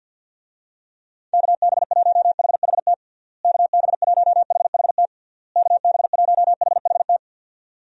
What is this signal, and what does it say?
Morse code, heard as a keyed tone, about 52 wpm, 700 Hz. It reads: KB1HHT KB1HHT KB1HHT
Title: USA Call Signs - 6 Characters - Repeated 3x - 50wpm